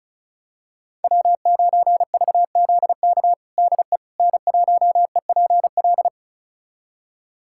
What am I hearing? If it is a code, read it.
Morse code, heard as a keyed tone, about 36 wpm, 700 Hz. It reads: W9VZK DE N1EPL
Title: USA Callsign DE Callsign - 5 Characters - 35wpm